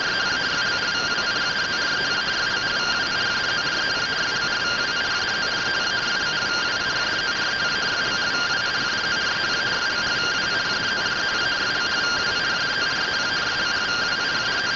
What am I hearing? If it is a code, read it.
Two-tone teleprinter signal, about 170 Hz shift.